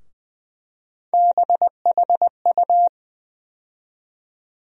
Morse code, heard as a keyed tone, about 20 wpm, 700 Hz. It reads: BHU
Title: USA Call Sign Postfixes - 3 Letters 20wpm